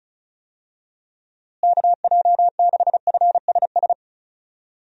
Morse code, heard as a keyed tone, about 36 wpm, 700 Hz. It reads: KJ6FSS
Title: USA Call Signs - 6 Characters - Repeated 2x - 35wpm